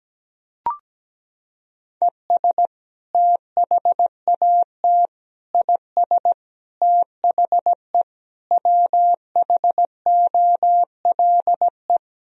Morse code, heard as a keyed tone, about 17 wpm, 700 Hz. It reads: ES THAT IS THE WHOLE
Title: Sentences from Top 400 Words and Abbreviations 17wpm